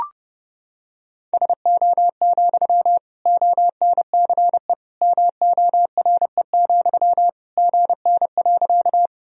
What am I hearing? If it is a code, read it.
Morse code, heard as a keyed tone, about 32 wpm, 700 Hz. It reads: SO, ONCE MORE, GN.
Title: Sentences from Top 300 Words and Abbreviations 30wpm